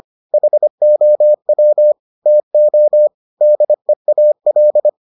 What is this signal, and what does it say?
Morse code, heard as a keyed tone, about 25 wpm, 600 Hz. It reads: HOW TO DEAL